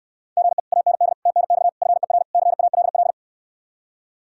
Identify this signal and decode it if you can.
Morse code, heard as a keyed tone, about 67 wpm, 700 Hz. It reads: GE AND TNX FER CALL